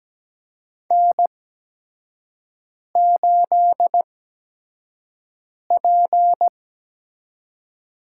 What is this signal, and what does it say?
Morse code, heard as a keyed tone, about 17 wpm, spaced as for 5 wpm, 700 Hz. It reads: N8P
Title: USA Call Signs - 4 Characters - 8x Character Spacing - 17wpm